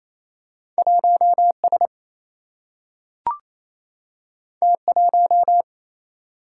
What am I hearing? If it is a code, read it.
Morse code, heard as a keyed tone, about 28 wpm, 700 Hz. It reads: 1S T1
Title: Single-Letter Single-Number 28wpm